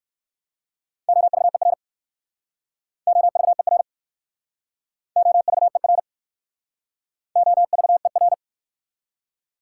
Morse code, heard as a keyed tone, about 60 wpm, 690 Hz. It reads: OVER OVER OVER OVER